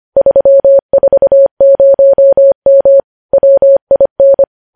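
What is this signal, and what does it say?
Morse code, heard as a keyed tone, about 25 wpm, 550 Hz. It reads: 340M WIN
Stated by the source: Morse code practice of news headlines sent at 25 wpm.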